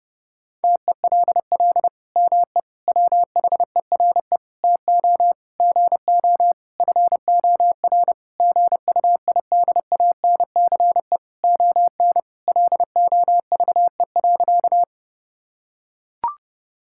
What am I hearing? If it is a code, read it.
Morse code, heard as a keyed tone, about 30 wpm, 700 Hz. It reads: TELL ME WHERE TO GO FOR GUIDANCE ON LOVE.
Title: Sentences from Top 1600 Words 30wpm